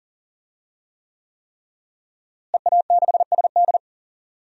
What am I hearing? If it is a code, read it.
Morse code, heard as a keyed tone, about 40 wpm, 700 Hz. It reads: EA6SD
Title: DXCC-100 Call Signs - 5 Characters - Repeated 3x - 40wpm